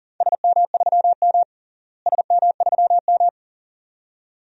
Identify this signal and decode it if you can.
Morse code, heard as a keyed tone, about 37 wpm, 700 Hz. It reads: SM3M SM3M
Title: DXCC-100 Call Signs - 4 Characters - Repeated 2x - 40wpm